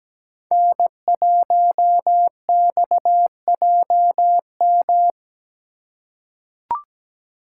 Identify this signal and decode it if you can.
Morse code, heard as a keyed tone, about 17 wpm, 700 Hz. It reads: N1XJM
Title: K1USN SST QSO Practice - 17wpm